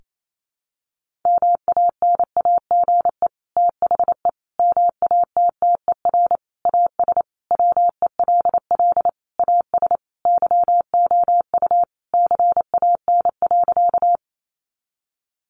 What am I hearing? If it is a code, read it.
Morse code, heard as a keyed tone, about 28 wpm, 700 Hz. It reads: MANAGE THE MATTER AS WELL AS YOU CAN.
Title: Sentences from Top 1400 Words 28wpm